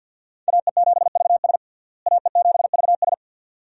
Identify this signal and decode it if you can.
Morse code, heard as a keyed tone, about 46 wpm, 700 Hz. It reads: AE7VS AE7VS
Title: USA Call Signs - 5 Characters - Repeated 2x - 50wpm